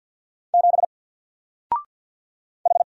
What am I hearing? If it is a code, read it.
Morse code, heard as a keyed tone, about 48 wpm, 700 Hz. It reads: 7 H
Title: Single Letter-Number Alt 50wpm